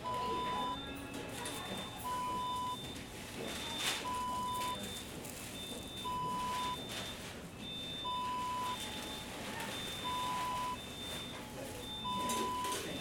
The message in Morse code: none